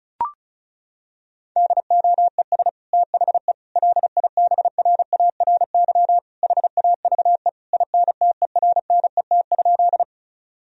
Morse code, heard as a keyed tone, about 36 wpm, 700 Hz. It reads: DOES THE LIBRARY HAVE INTERNET?
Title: Sentences from Top 1000 Words - Review Part 4 - 35wpm